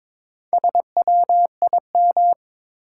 Morse code, heard as a keyed tone, about 23 wpm, 700 Hz. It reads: SWIM
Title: Four Letter Words - 22wpm